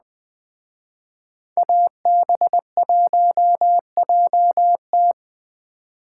Morse code, heard as a keyed tone, about 20 wpm, 700 Hz. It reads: AB1JT
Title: Speed Racing - USA 2x2 Call Signs 17wpm